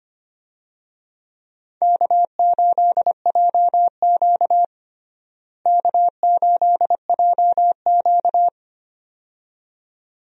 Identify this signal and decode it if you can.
Morse code, heard as a keyed tone, about 25 wpm, 700 Hz. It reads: K8JQ K8JQ
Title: USA Call Signs - 4 Characters - Repeated 2x - 25wpm